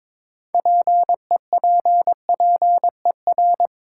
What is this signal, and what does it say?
Morse code, heard as a keyed tone, about 22 wpm, 700 Hz. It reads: PEPPER
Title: Sets of 5 Words - New Word Drill - 22wpm